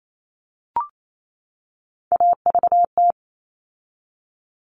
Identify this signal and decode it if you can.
Morse code, heard as a keyed tone, about 28 wpm, 700 Hz. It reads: AVT
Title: All Cut Numbers - 3-Digit 28wpm